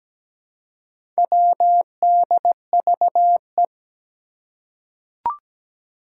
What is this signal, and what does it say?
Morse code, heard as a keyed tone, about 17 wpm, 700 Hz. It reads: WDVE
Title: Four Challenging Letter-Number Combinations 17wpm